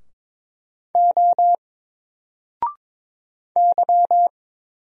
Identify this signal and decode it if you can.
Morse code, heard as a keyed tone, about 22 wpm, 700 Hz. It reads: O Y
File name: Single-Letter-Number-22wpm